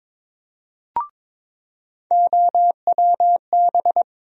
Morse code, heard as a keyed tone, about 22 wpm, 700 Hz. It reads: OWB
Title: USA Call Sign Postfixes - 3 Letters 22wpm